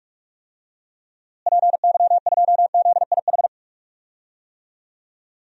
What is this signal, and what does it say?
Morse code, heard as a keyed tone, about 46 wpm, 700 Hz. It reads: PY2ZIH
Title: DXCC-100 Call Signs - 6 Characters - Repeated 3x - 45wpm